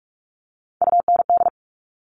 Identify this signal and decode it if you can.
Morse code, heard as a keyed tone, about 46 wpm, 700 Hz. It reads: UND
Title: Top Three Letter Combinations in Words 45wpm